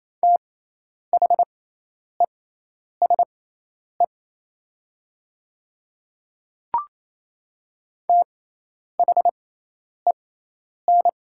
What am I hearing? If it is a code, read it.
Morse code, heard as a keyed tone, about 27 wpm, spaced as for 10 wpm, 690 Hz. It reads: THESE THEN
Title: 6x Character Spacing - Top 100 Words - 28wpm